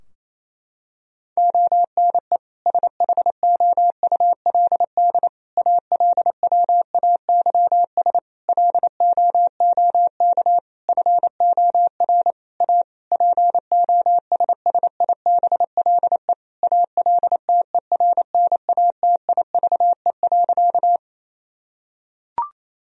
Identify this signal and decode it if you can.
Morse code, heard as a keyed tone, about 27 wpm, 700 Hz. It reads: ONE SHOULD ALWAYS LOOK FOR A POSSIBLE ALTERNATIVE.
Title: Sentences from Top 1000 Words - Review Part 4 - 28wpm